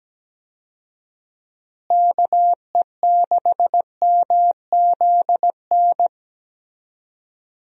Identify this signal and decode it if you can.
Morse code, heard as a keyed tone, about 17 wpm, 700 Hz. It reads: KE6MZN
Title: USA Call Signs - 6 Characters - Repeated 3x - 17wpm